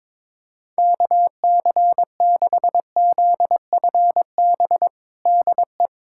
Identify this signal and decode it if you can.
Morse code, heard as a keyed tone, about 22 wpm, 700 Hz. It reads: KC6ZFB DE
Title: USA Callsign DE Callsign 22wpm